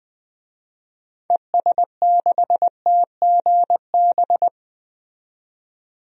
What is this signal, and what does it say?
Morse code, heard as a keyed tone, about 20 wpm, 700 Hz. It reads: ES6TGB
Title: DXCC-100 Call Signs - 6 Characters - 20wpm